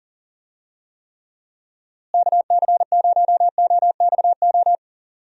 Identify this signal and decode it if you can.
Morse code, heard as a keyed tone, about 40 wpm, 700 Hz. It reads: KC0OXO
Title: USA Call Signs - 6 Characters - Repeated 2x - 40wpm